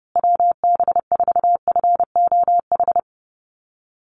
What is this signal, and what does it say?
Morse code, heard as a keyed tone, about 30 wpm, 700 Hz. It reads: WB4FOH
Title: USA Call Signs Repeated 3x 30wpm